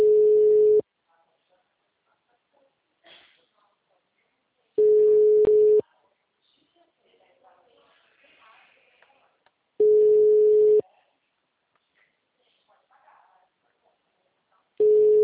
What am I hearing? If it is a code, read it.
Ringback tone (Europe, 425 Hz), with speech.